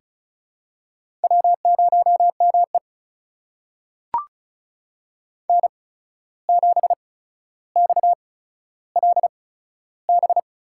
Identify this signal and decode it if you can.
Morse code, heard as a keyed tone, about 36 wpm, spaced as for 14 wpm, 700 Hz. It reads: W0ME N7XLB
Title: USA Call Signs - 8x Character Spacing - 35wpm